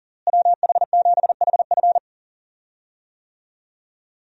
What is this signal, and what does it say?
Morse code, heard as a keyed tone, about 39 wpm, 700 Hz. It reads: WH7HF
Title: USA Call Signs - 5 Characters - Repeated 3x - 40wpm